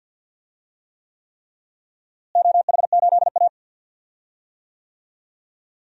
Morse code, heard as a keyed tone, about 50 wpm, 700 Hz. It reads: OH8A
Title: DXCC-100 Call Signs - 4 Characters - Repeated 3x - 50wpm